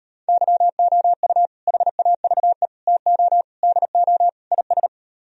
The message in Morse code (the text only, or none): YOU HAVE TO DO IS